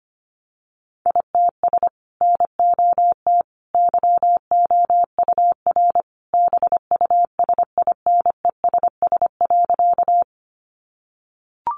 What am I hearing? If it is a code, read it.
Morse code, heard as a keyed tone, about 25 wpm, 700 Hz. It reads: ITS NOT YOUR BUSINESS.
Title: Sentences from Top 600 Words 25wpm